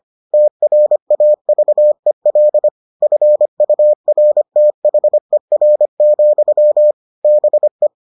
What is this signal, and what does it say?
Morse code, heard as a keyed tone, about 25 wpm, 600 Hz. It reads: TRAVEL FURTHER, BE